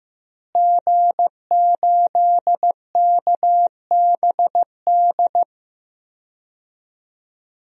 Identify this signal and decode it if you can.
Morse code, heard as a keyed tone, about 15 wpm, 700 Hz. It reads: G8KBD
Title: DXCC-100 Call Signs - 5 Characters - Repeated 2x - 15wpm